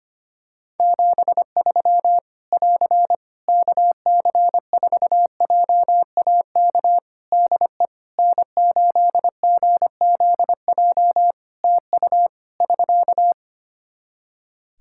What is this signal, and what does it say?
Morse code, heard as a keyed tone, about 25 wpm, 700 Hz. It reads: 73 <AR> KC4JAK DE N8GZJ TU <SK>
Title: Practice QSO - Exchange 4 - Ending - 25wpm